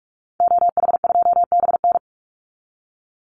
Morse code, heard as a keyed tone, about 44 wpm, 700 Hz. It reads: OH2BN
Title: Call Signs Repeated 2x - 45wpm